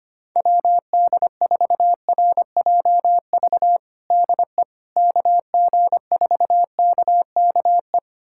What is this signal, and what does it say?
Morse code, heard as a keyed tone, about 25 wpm, 700 Hz. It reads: WD4RJV DE KG4KKE
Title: USA Callsign DE Callsign - 6 Characters - 25wpm